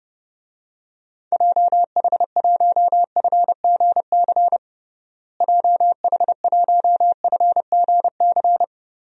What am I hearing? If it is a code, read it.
Morse code, heard as a keyed tone, about 30 wpm, 700 Hz. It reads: JH1FGC JH1FGC
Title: DXCC-100 Call Signs - 6 Characters - Repeated 3x - 30wpm